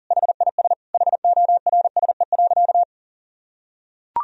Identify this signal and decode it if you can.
Morse code, heard as a keyed tone, about 37 wpm, 700 Hz. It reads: HIS HORSE.
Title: Sentences from Top 1000 Words - Review Part 1 - 40wpm